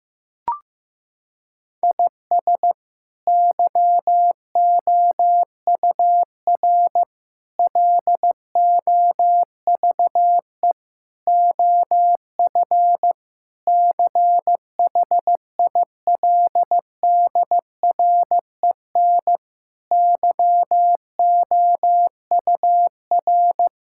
Morse code, heard as a keyed tone, about 15 wpm, 700 Hz. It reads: IS YOUR LOVE OF CHILDREN YOUR
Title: Sentences from Top 1000 Words - Review Part 5 - 15wpm